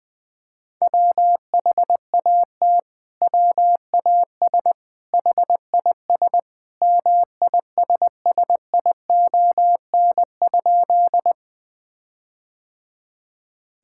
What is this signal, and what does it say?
Morse code, heard as a keyed tone, about 21 wpm, 700 Hz. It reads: WHAT WAS HIS MISSION?
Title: Sentences from Top 1600 Words 20wpm